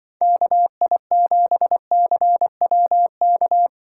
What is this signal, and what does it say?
Morse code, heard as a keyed tone, about 24 wpm, 700 Hz. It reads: KI7CWK
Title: Speed Racing - USA 2x3 Call Signs 20wpm